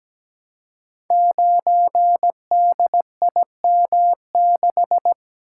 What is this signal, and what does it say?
Morse code, heard as a keyed tone, about 17 wpm, 700 Hz. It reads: 9DIM6
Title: Code Groups - 5 Letter:Numbers - 17wpm